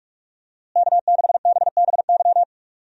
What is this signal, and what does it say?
Morse code, heard as a keyed tone, about 46 wpm, 700 Hz. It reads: K6BBY
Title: USA Call Signs - 5 Characters - Repeated 2x - 45wpm